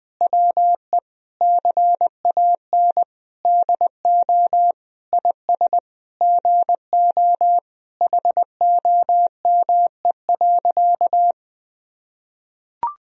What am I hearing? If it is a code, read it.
Morse code, heard as a keyed tone, about 20 wpm, 700 Hz. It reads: WE CAN DO IS GO HOME.
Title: Sentences from Top 200 Words and Abbreviations 20wpm